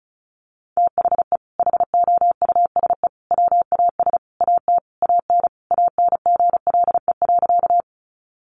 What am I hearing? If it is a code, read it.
Morse code, heard as a keyed tone, about 34 wpm, 700 Hz. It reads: THE HOUSE WAS AT AN ANGLE.
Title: Sentences from Top 1500 Words 35wpm